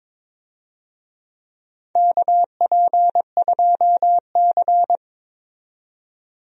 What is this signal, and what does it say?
Morse code, heard as a keyed tone, about 22 wpm, 700 Hz. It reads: KP2C
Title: DXCC-100 Call Signs - 4 Characters - 22wpm